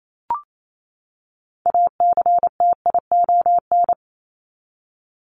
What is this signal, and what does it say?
Morse code, heard as a keyed tone, about 28 wpm, 700 Hz. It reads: ACTION
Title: Top 501-750 Words 28wpm